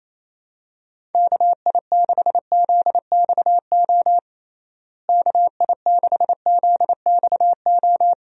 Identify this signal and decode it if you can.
Morse code, heard as a keyed tone, about 28 wpm, 700 Hz. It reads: KI6ZXO KI6ZXO
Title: USA Call Signs - 6 Characters - Repeated 3x - 28wpm